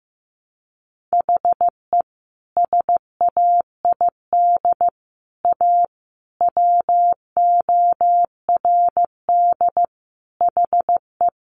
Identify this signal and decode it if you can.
Morse code, heard as a keyed tone, about 15 wpm, 700 Hz. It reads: HE SAID A WORD HE
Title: Sentences from Top 600 Words 15wpm